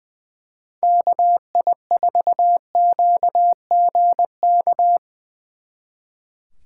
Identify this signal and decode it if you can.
Morse code, heard as a keyed tone, about 22 wpm, 700 Hz. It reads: KI4QGK
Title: Speed Racing - USA 2x3 Call Signs 20wpm